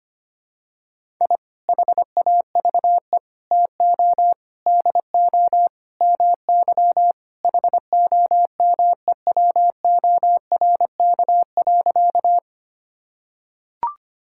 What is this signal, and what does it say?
Morse code, heard as a keyed tone, about 25 wpm, 700 Hz. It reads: I HAVE TO DO MY HOMEWORK.
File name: practice / 25wpm / Sentences with 3-Word Phrases - 25wpm